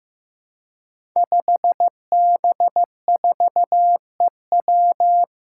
Morse code, heard as a keyed tone, about 15 wpm, 700 Hz. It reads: 5B4EW
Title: Five Challenging Letter-Number Combinations 15wpm